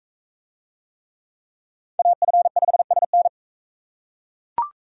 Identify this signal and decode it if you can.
Morse code, heard as a keyed tone, about 40 wpm, 700 Hz. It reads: AF5SN